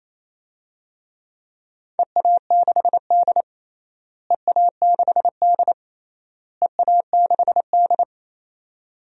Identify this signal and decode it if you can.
Morse code, heard as a keyed tone, about 28 wpm, 700 Hz. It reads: EA6D EA6D EA6D
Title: DXCC-100 Call Signs - 4 Characters - Repeated 3x - 28wpm